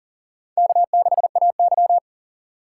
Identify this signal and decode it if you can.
Morse code, heard as a keyed tone, about 38 wpm, 690 Hz. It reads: K6AY
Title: USA Call Signs - 4 Characters - 2x Character Spacing - 40wpm